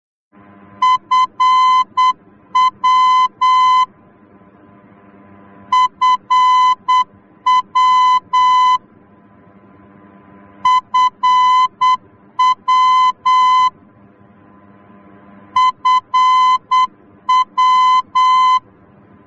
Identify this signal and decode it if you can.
Morse code, heard as a keyed tone, about 8 wpm, 1020 Hz. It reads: FW FW FW FW